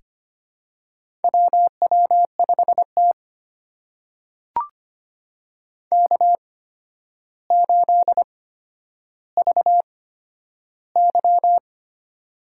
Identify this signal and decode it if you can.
Morse code, heard as a keyed tone, about 25 wpm, spaced as for 11 wpm, 700 Hz. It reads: WW5T K8VY
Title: USA Call Signs - 4 Characters - 8x Character Spacing - 25wpm